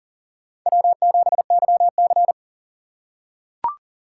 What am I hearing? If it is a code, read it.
Morse code, heard as a keyed tone, about 40 wpm, 700 Hz. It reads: W7YC
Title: USA Call Signs - 4 Characters - 8x Character Spacing - 40wpm